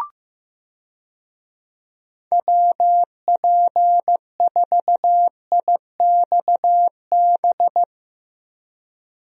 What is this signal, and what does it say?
Morse code, heard as a keyed tone, about 16 wpm, 700 Hz. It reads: WP4IXB
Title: USA Call Signs - 6 Characters - Repeated 2x - 15wpm